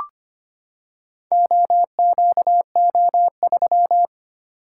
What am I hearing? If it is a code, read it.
Morse code, heard as a keyed tone, about 25 wpm, 700 Hz. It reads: OQO3
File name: practice / 25wpm / Four Letter-Number Combinations - 25wpm